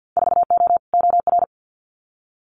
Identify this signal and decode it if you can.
Morse code, heard as a keyed tone, about 50 wpm, 700 Hz. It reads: 4O OR
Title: Winter Field Day Exchanges Encore 50wpm